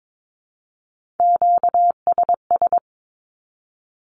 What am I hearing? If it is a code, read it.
Morse code, heard as a keyed tone, about 22 wpm, 700 Hz. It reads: QSS
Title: Q-Codes 22wpm